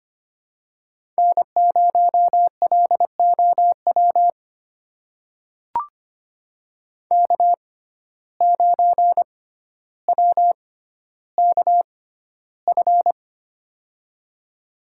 Morse code, heard as a keyed tone, about 25 wpm, spaced as for 13 wpm, 700 Hz. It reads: N0LOW K9WKF
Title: USA Call Signs - 5 Characters - 6x Character Spacing - 25wpm